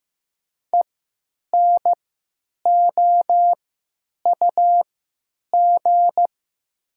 Morse code, heard as a keyed tone, about 15 wpm, spaced as for 8 wpm, 700 Hz. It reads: ENOUG
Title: 3x Character Spacing - Top 500 Words - 15wpm